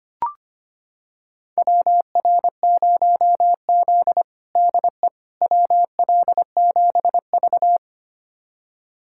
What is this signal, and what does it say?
Morse code, heard as a keyed tone, about 25 wpm, 700 Hz. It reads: WR0Z DE WL7V
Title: USA Callsign DE Callsign - 4 Characters - 25wpm